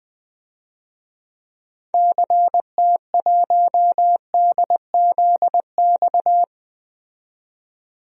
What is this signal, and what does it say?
Morse code, heard as a keyed tone, about 20 wpm, 700 Hz. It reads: CT1DZX
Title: DXCC-100 Call Signs - 6 Characters - 20wpm